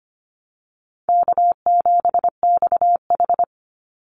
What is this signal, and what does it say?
Morse code, heard as a keyed tone, about 25 wpm, 700 Hz. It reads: K7XH
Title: Easy Call Signs - 25wpm